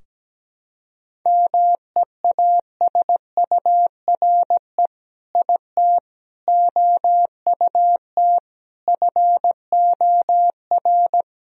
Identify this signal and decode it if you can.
Morse code, heard as a keyed tone, about 17 wpm, 700 Hz. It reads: MEASURE IT OUT FOR
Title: Sentences from Top 1000 Words - Review Part 2 - 17wpm